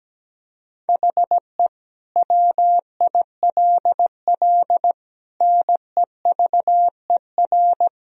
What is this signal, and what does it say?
Morse code, heard as a keyed tone, about 17 wpm, 700 Hz. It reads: HE WILL NEVER
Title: Sentences from Top 1000 Words - Review Part 2 - 17wpm